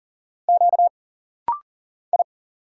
Morse code, heard as a keyed tone, about 40 wpm, 700 Hz. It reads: Q I
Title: Single Letters - Rapid-Fire 40wpm